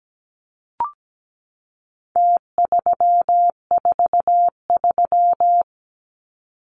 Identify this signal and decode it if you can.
Morse code, heard as a keyed tone, about 17 wpm, 700 Hz. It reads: T343
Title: Numbers - Contest Style - 4-Digit 17wpm